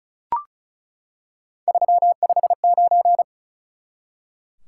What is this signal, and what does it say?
Morse code, heard as a keyed tone, about 36 wpm, 700 Hz. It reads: 359
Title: Top 180 QSO Elements 35wpm